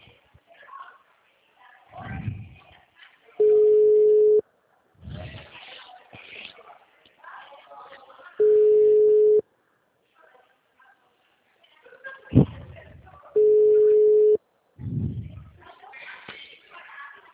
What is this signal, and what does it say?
Ringback tone (Europe, 425 Hz), with speech.